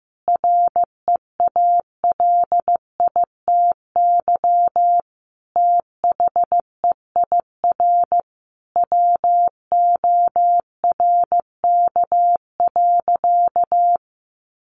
Morse code, heard as a keyed tone, about 15 wpm, 700 Hz. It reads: REALITY THEIR WORK.
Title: Sentences from Top 1400 Words 15wpm